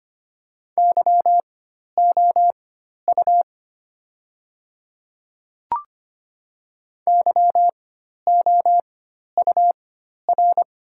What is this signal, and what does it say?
Morse code, heard as a keyed tone, about 25 wpm, spaced as for 12 wpm, 700 Hz. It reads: YOU YOUR
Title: 4x Character Spacing - Top 500 Words - 25wpm